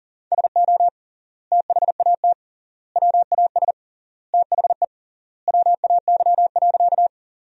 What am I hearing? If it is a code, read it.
Morse code, heard as a keyed tone, about 40 wpm, 700 Hz. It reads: SO THAT WAS THE WAY.
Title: 3x Word-Spacing - Sentences from Top 100 Words - 40wpm